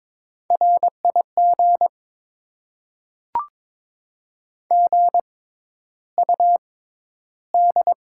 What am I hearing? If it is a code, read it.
Morse code, heard as a keyed tone, about 21 wpm, spaced as for 11 wpm, 700 Hz. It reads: RIG GUD
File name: Top 20 QSO Elements - 6x Character Spacing - 22wpm